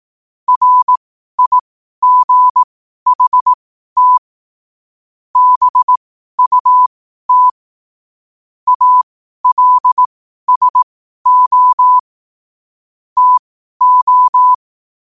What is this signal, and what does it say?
Morse code, heard as a keyed tone, about 18 wpm, spaced as for 12 wpm, 1000 Hz. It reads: RIGHT BUT ALSO TO
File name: qotdfiles / 12wpm